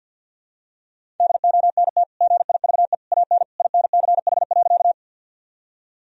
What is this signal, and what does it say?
Morse code, heard as a keyed tone, about 50 wpm, 700 Hz. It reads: DONT GIVE AN INCH.
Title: Sentences from Top 1000 Words - Review Part 3 - 50wpm